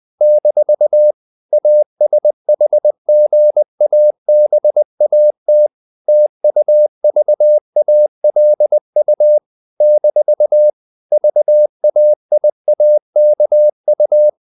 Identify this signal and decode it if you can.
Morse code, heard as a keyed tone, about 20 wpm, 600 Hz. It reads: - ASHGABAT TUVALU - VAIAKU
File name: CW / Practice / NationalCapitals20